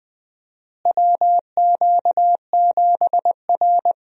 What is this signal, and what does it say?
Morse code, heard as a keyed tone, about 20 wpm, 700 Hz. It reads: WQ7R
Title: USA Call Signs - 4 Characters - Repeated 3x - 20wpm